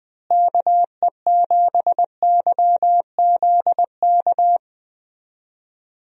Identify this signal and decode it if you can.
Morse code, heard as a keyed tone, about 21 wpm, 700 Hz. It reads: KE7YZK
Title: USA Call Signs - 6 Characters - Repeated 2x - 20wpm